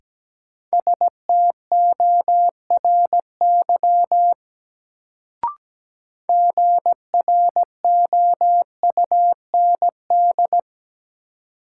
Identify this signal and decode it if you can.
Morse code, heard as a keyed tone, about 17 wpm, 700 Hz. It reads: STORY GROUND
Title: Sets of 3 Words - New Word Drill - 17wpm